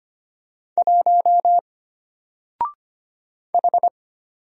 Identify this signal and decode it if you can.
Morse code, heard as a keyed tone, about 25 wpm, 700 Hz. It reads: 1 H
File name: Single-Letter-Number-25wpm